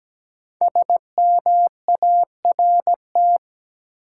Morse code, heard as a keyed tone, about 17 wpm, 700 Hz. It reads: SMART
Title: Five Letter Words - 17wpm